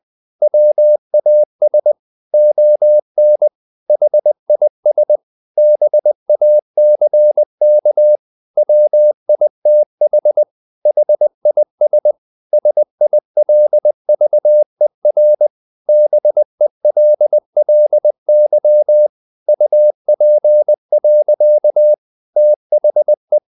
Morse code, heard as a keyed tone, about 20 wpm, 600 Hz. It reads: WAS ON HIS BACK WITH HIS SILVER BELLY UP. THE
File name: the old man and the sea part 2 20wpm